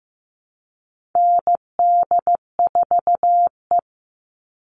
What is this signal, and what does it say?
Morse code, heard as a keyed tone, about 15 wpm, 700 Hz. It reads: ND4E
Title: All Cut Numbers - 4-Digit 15wpm